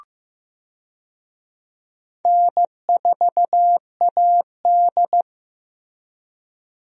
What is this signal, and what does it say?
Morse code, heard as a keyed tone, about 15 wpm, 700 Hz. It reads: N4AD
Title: USA Call Signs - 4 Characters - Repeated 3x - 15wpm